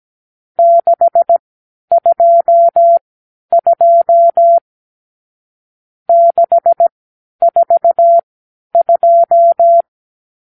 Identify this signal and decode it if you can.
Morse code, heard as a keyed tone, about 17 wpm, spaced as for 10 wpm, 680 Hz. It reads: 622 642
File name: k7qo / Lesson 50 (1-6)